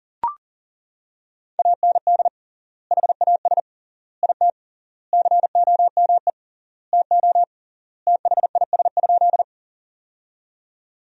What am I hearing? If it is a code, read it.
Morse code, heard as a keyed tone, about 40 wpm, 700 Hz. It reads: AND HAS IT COME TO THIS?
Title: 3x Word-Spacing - Sentences from Top 100 Words - 40wpm